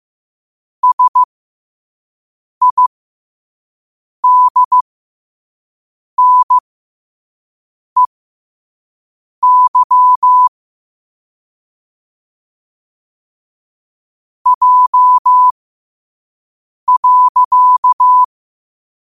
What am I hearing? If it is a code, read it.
Morse code, heard as a keyed tone, about 15 wpm, spaced as for 5 wpm, 1000 Hz. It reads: SIDNEY J.
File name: qotdfiles / 5wpm